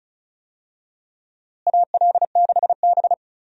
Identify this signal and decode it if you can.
Morse code, heard as a keyed tone, about 36 wpm, 700 Hz. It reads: AL6B
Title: USA Call Signs - 4 Characters - Repeated 3x - 35wpm